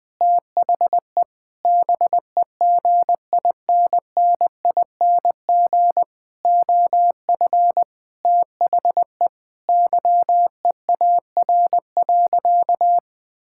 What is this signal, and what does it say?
Morse code, heard as a keyed tone, about 20 wpm, 700 Hz. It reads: THE BEGINNING OF THE YEAR.
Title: Sentences with 6-Word Phrases - 20wpm